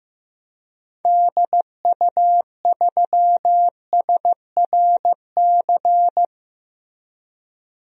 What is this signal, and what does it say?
Morse code, heard as a keyed tone, about 15 wpm, 700 Hz. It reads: DU3SRC
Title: DXCC-100 Call Signs - 6 Characters - 15wpm